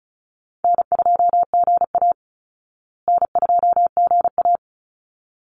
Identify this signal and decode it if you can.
Morse code, heard as a keyed tone, about 34 wpm, 700 Hz. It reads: N2GA N2GA
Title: Easy Call Signs Repeated 2x - 35wpm